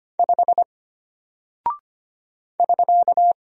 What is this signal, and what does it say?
Morse code, heard as a keyed tone, about 25 wpm, 700 Hz. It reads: 5 <SK>
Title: Single Character (Letter:Number:Punctuation:Prosign) Alt 25wpm